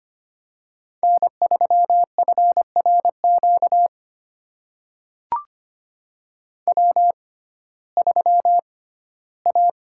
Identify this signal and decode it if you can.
Morse code, heard as a keyed tone, about 25 wpm, spaced as for 15 wpm, 700 Hz. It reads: N3FRQ W3A
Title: USA Call Signs - 5 Characters - 6x Character Spacing - 25wpm